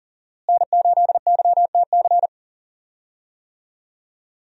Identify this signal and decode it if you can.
Morse code, heard as a keyed tone, about 39 wpm, 700 Hz. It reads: N8YTC
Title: USA Call Signs - 5 Characters - Repeated 2x - 40wpm